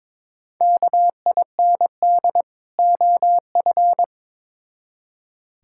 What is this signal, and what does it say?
Morse code, heard as a keyed tone, about 21 wpm, 700 Hz. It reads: KIND OF
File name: Build A Phrase - 2-Word Phrases - 22wpm